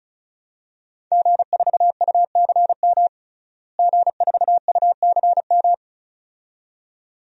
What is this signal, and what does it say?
Morse code, heard as a keyed tone, about 34 wpm, 700 Hz. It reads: G4UCM G4UCM
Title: DXCC-100 Call Signs - 5 Characters - Repeated 2x - 35wpm